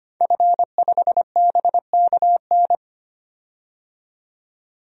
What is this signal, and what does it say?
Morse code, heard as a keyed tone, about 25 wpm, 700 Hz. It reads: F5BKN
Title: DXCC-100 Call Signs - 5 Characters - 25wpm